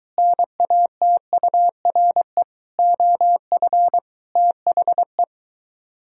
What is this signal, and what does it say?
Morse code, heard as a keyed tone, about 23 wpm, 700 Hz. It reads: NATURE OF THE